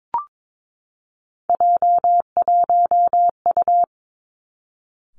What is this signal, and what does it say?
Morse code, heard as a keyed tone, about 22 wpm, 700 Hz. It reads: J1U
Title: Three Challenging Letter-Number Combinations 22wpm